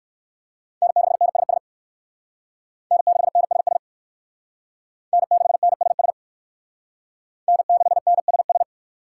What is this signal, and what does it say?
Morse code, heard as a keyed tone, about 57 wpm, 700 Hz. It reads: N6NSS N6NSS N6NSS N6NSS